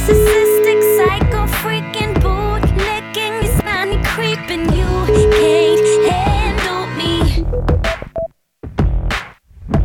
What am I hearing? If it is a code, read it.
Ringback tone (Europe, 425 Hz), with noise.